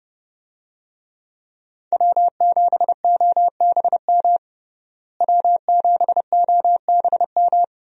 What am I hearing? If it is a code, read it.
Morse code, heard as a keyed tone, about 30 wpm, 700 Hz. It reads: W7OBM W7OBM
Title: USA Call Signs - 5 Characters - Repeated 2x - 30wpm